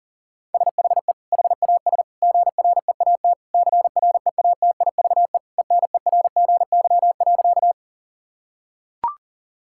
Morse code, heard as a keyed tone, about 37 wpm, 700 Hz. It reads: SHE HAS GREAT CREATIVE ENERGY.
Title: Sentences from Top 1000 Words - Review Part 6 - 40wpm